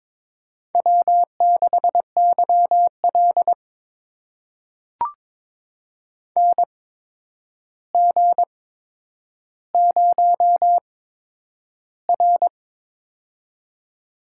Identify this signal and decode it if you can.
Morse code, heard as a keyed tone, about 22 wpm, spaced as for 9 wpm, 700 Hz. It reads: W6YL NG0R
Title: USA Call Signs - 4 Characters - 8x Character Spacing - 22wpm